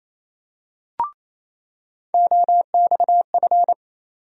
Morse code, heard as a keyed tone, about 28 wpm, 700 Hz. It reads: OXF
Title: USA Call Sign Postfixes - 3 Letters 28wpm